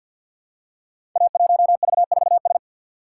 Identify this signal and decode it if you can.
Morse code, heard as a keyed tone, about 48 wpm, 700 Hz. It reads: A1V4S
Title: Code Groups - 5 Letter:Numbers - 50wpm